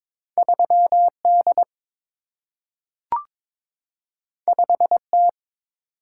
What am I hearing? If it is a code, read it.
Morse code, heard as a keyed tone, about 22 wpm, 700 Hz. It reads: 3D 5T
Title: Single-Letter Single-Number 22wpm